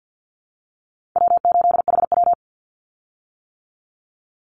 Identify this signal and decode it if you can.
Morse code, heard as a keyed tone, about 50 wpm, 700 Hz. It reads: W8HW
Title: Call Signs - 50wpm